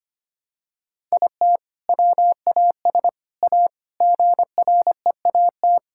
Morse code, heard as a keyed tone, about 25 wpm, 700 Hz. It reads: IT WAS A GREAT
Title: Sentences from Top 1000 Words - Review Part 5 - 25wpm